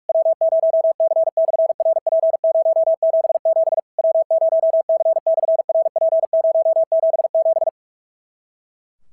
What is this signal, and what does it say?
Morse code, heard as a keyed tone, about 44 wpm, 650 Hz. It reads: W0C/RP077 W0C/RP077
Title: SOTA QSO Practice 45wpm